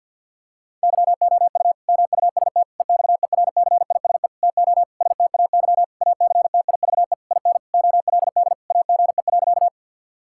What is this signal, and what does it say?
Morse code, heard as a keyed tone, about 50 wpm, 700 Hz. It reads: YOU MUST EXERCISE TO STAY ACTIVE IN OLD AGE.
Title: Sentences from Top 1000 Words - Review Part 6 - 50wpm